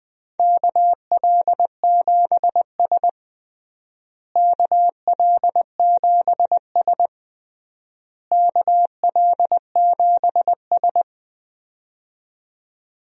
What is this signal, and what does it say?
Morse code, heard as a keyed tone, about 20 wpm, 700 Hz. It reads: KL7S KL7S KL7S
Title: DXCC-100 Call Signs - 4 Characters - Repeated 3x - 20wpm